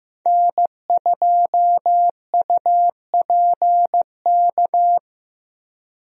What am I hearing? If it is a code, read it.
Morse code, heard as a keyed tone, about 15 wpm, 700 Hz. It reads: N2UPK
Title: USA Call Signs - 5 Characters - Repeated 3x - 15wpm